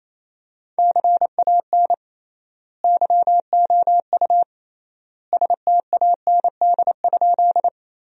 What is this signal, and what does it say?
Morse code, heard as a keyed tone, about 28 wpm, 700 Hz. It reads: CAN YOU STAND?
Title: 3x Word-Spacing - Sentences from Top 300 Words - 28wpm